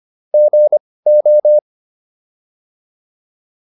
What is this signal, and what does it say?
Morse code, heard as a keyed tone, about 25 wpm, spaced as for 18 wpm, 600 Hz. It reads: GO
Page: intermediate-practice-files